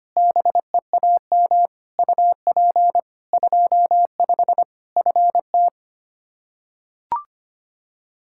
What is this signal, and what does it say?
Morse code, heard as a keyed tone, about 25 wpm, 700 Hz. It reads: BEAM UP 25 FT
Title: Practice QSO - Exchange 2 - Info - 25wpm